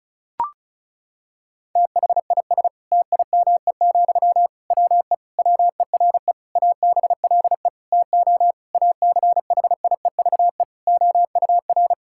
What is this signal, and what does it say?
Morse code, heard as a keyed tone, about 36 wpm, 700 Hz. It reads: THIS TIME, WE WERE ABLE TO ACHIEVE OUR
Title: Sentences from Top 1000 Words - Review Part 6 - 35wpm